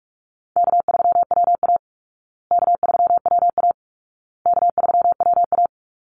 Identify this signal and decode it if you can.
Morse code, heard as a keyed tone, about 44 wpm, 700 Hz. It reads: K3WA K3WA K3WA
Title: Call Signs Repeated 2x - 45wpm